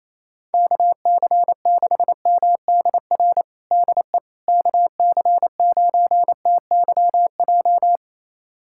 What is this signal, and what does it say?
Morse code, heard as a keyed tone, about 27 wpm, 700 Hz. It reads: KC6MDR DE KC9TYJ
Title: USA Callsign DE Callsign - 6 Characters - 28wpm